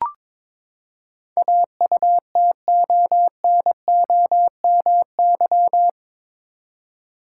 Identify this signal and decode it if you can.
Morse code, heard as a keyed tone, about 22 wpm, 700 Hz. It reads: AUTONOMY
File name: practice / 22wpm / Eight Letter Words - 22wpm